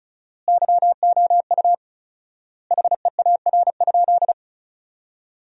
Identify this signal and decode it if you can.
Morse code, heard as a keyed tone, about 36 wpm, 700 Hz. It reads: YOU HEAR?
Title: 4x Word-Spacing - Sentences from Top 300 Words - 35wpm